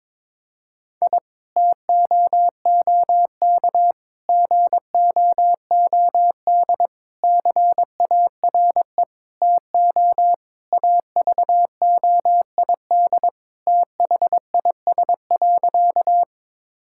Morse code, heard as a keyed tone, about 22 wpm, 700 Hz. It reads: I TOOK GOOD CARE TO AVOID THIS.
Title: Sentences from Top 1000 Words - Review Part 6 - 22wpm